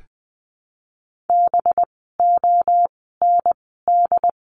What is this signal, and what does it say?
Morse code, heard as a keyed tone, about 20 wpm, spaced as for 14 wpm, 700 Hz. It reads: BOND
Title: 185 Lesson 30f - Letter b - Words 20wpm